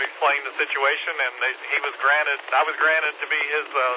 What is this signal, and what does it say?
Emergency Alert System (SAME) tones, heard as harsh buzzing bursts, with speech. It is heard from 0.9 to 3.5 s.